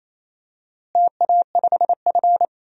Morse code, heard as a keyed tone, about 28 wpm, 700 Hz. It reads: TA5F
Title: DXCC-100 Call Signs - 4 Characters - Repeated 2x - 28wpm